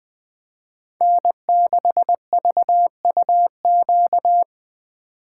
Morse code, heard as a keyed tone, about 20 wpm, 700 Hz. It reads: N6VUQ
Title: USA Call Signs - 5 Characters - 4x Character Spacing - 20wpm